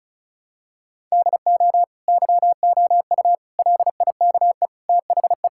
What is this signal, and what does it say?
Morse code, heard as a keyed tone, about 36 wpm, 700 Hz. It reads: DO YOU LIKE THE
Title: Sentences from Top 1000 Words - Review Part 6 - 35wpm